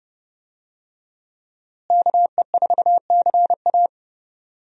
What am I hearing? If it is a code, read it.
Morse code, heard as a keyed tone, about 30 wpm, 700 Hz. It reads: KE4CA
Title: USA Call Signs - 5 Characters - Repeated 2x - 30wpm